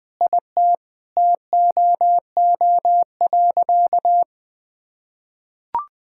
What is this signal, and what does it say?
Morse code, heard as a keyed tone, about 20 wpm, 700 Hz. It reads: IT TOO.
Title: Sentences with 9-Word Phrases - 20wpm